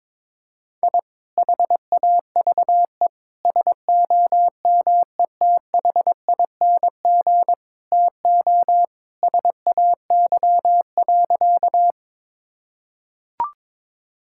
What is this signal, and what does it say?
Morse code, heard as a keyed tone, about 22 wpm, 700 Hz. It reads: I HAVE SOMETHING TO SAY.
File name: practice / 22wpm / Sentences with 2-Word Phrases - 22wpm